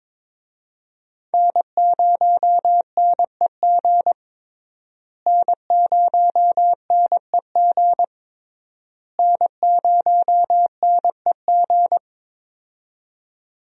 Morse code, heard as a keyed tone, about 22 wpm, 700 Hz. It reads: N0NEG N0NEG N0NEG
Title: USA Call Signs - 5 Characters - Repeated 3x - 22wpm